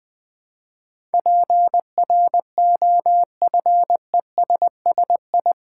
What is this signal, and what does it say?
Morse code, heard as a keyed tone, about 20 wpm, 700 Hz. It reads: PROFESSI
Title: Build A Word - 4 Letters At A Time - 12-Letter Words - 20wpm